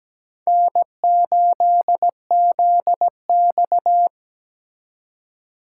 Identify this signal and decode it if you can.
Morse code, heard as a keyed tone, about 20 wpm, 700 Hz. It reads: N8ZX
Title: Speed Racing - USA 1x2 Call Signs 17wpm